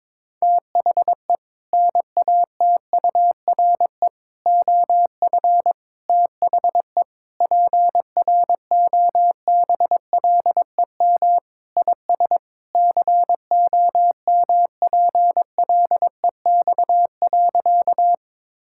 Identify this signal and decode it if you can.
Morse code, heard as a keyed tone, about 22 wpm, 700 Hz. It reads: THE NATURE OF THE PROBLEM IS COMPLEX.
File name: practice / 22wpm / Sentences with 4-Word Phrases - 22wpm